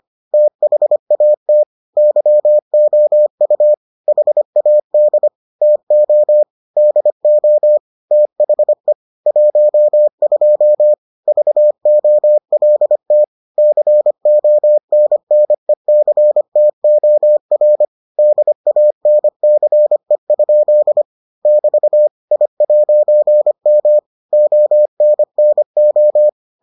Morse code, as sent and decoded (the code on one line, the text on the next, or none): - .... .- - / -.-- --- ..- / .... .- -.. / - --- / -.. --- / - .... / .---- ..--- / ...- --- .-.. - / -.-. --- -. -. -.-. - --- .-. / -.. .- -. -.-. ..--.. / -...- / .. .----. -- / --- -. -. ---
THAT YOU HAD TO DO THE 12 VOLT CONNECTOR DANCE? <BT> I'M ONNO